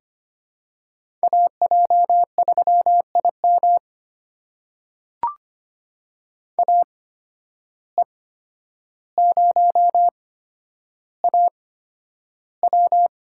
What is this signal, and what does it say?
Morse code, heard as a keyed tone, about 25 wpm, spaced as for 11 wpm, 700 Hz. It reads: AJ3IM AE0AW
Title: USA Call Signs - 5 Characters - 8x Character Spacing - 25wpm